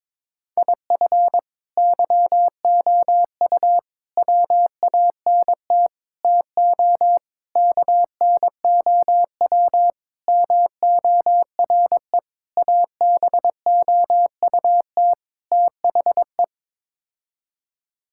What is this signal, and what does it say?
Morse code, heard as a keyed tone, about 22 wpm, 700 Hz. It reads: IF YOU WANT TO KNOW MORE ABOUT THE
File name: practice / 22wpm / Sets of 8 Words - 22wpm